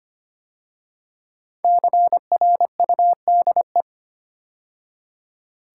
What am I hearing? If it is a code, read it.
Morse code, heard as a keyed tone, about 25 wpm, 700 Hz. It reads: CRUDE
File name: 5 Letter Words - First 3 Letters Given - 25wpm